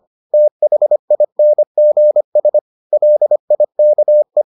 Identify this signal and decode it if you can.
Morse code, heard as a keyed tone, about 25 wpm, 600 Hz. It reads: THINGS LIKE